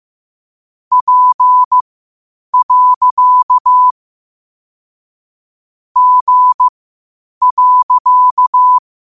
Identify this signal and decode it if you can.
Morse code, heard as a keyed tone, about 15 wpm, spaced as for 8 wpm, 1000 Hz. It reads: P. G.
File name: qotdfiles / 8wpm